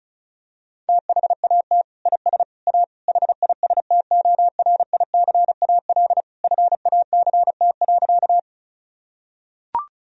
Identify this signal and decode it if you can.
Morse code, heard as a keyed tone, about 36 wpm, 700 Hz. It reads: THAT IS A HISTORICAL FACT.
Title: Sentences from Top 1000 Words - Review Part 4 - 35wpm